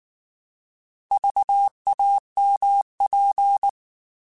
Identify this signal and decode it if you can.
Morse code, heard as a keyed tone, about 19 wpm, 800 Hz. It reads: VAMP